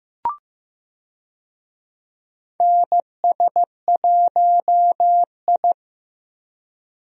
Morse code, heard as a keyed tone, about 15 wpm, 700 Hz. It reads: NS1I
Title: USA Call Signs - 4 Characters - Repeated 3x - 15wpm